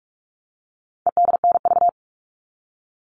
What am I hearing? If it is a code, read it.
Morse code, heard as a keyed tone, about 44 wpm, 700 Hz. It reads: EDNV
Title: All Cut Numbers - 4-Digit 45wpm